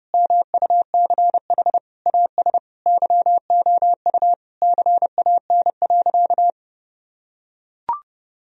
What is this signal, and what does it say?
Morse code, heard as a keyed tone, about 30 wpm, 700 Hz. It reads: MUCH AS YOU CAN.
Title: Sentences from Top 1000 Words - Review Part 1 - 30wpm